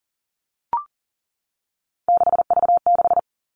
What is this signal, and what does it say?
Morse code, heard as a keyed tone, about 39 wpm, 700 Hz. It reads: 6V6
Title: Three Challenging Letter-Number Combinations 40wpm